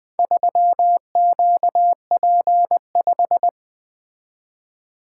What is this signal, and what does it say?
Morse code, heard as a keyed tone, about 20 wpm, 700 Hz. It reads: 3QP5
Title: Four Letter-Number Combinations - 20wpm